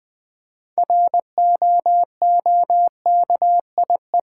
Morse code, heard as a keyed tone, about 20 wpm, 700 Hz. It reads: ROOKIE
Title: 6 Letter Words - First 3 Letters Given - 20wpm